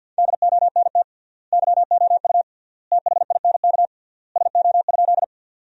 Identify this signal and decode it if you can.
Morse code, heard as a keyed tone, about 52 wpm, 700 Hz. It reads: DONT YOU THINK SO?
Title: 3x Word-Spacing - Sentences from Top 300 Words - 50wpm